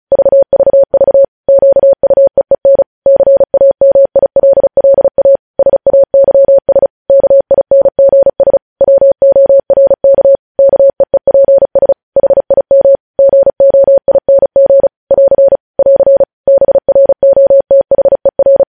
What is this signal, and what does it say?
Morse code, heard as a keyed tone, about 36 wpm, 550 Hz. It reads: VVV QUEEN CAMILLA SAYS KINGS WORK KEEPS HIM GOING <AR> <AR> BROTHER
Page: News Headlines in Morse Code at 35 WPM